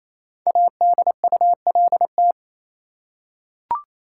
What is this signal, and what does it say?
Morse code, heard as a keyed tone, about 28 wpm, 700 Hz. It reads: ADULT
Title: Five Letter Words - 28wpm